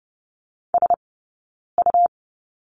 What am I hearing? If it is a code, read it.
Morse code, heard as a keyed tone, about 30 wpm, spaced as for 9 wpm, 690 Hz. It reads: SU
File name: Lesson 19d - Letter w - Two Characters 30wpm